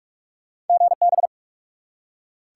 Morse code, heard as a keyed tone, about 44 wpm, 700 Hz. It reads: GB
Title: Top 180 QSO Elements 45wpm